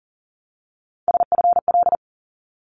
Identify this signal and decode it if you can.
Morse code, heard as a keyed tone, about 40 wpm, 700 Hz. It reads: SFL
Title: ARRL Sections 40wpm